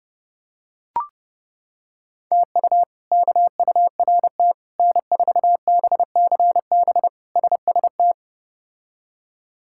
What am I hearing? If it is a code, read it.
Morse code, heard as a keyed tone, about 30 wpm, 700 Hz. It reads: TU KURT N4BCB SST
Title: K1USN SST QSO Practice - 30wpm